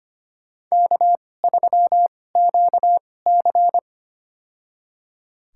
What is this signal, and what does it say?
Morse code, heard as a keyed tone, about 25 wpm, spaced as for 18 wpm, 700 Hz. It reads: K3QC
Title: USA Call Signs - 4 Characters - 2x Character Spacing - 25wpm